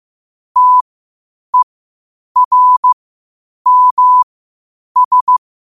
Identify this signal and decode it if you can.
Morse code, heard as a keyed tone, about 15 wpm, spaced as for 8 wpm, 1000 Hz. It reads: TERMS
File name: qotdfiles / 8wpm